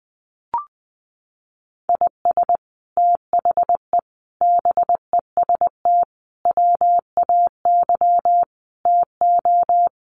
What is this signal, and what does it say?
Morse code, heard as a keyed tone, about 20 wpm, 700 Hz. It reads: IS THE BEST WAY TO
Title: Sets of 5 Words Encore 20wpm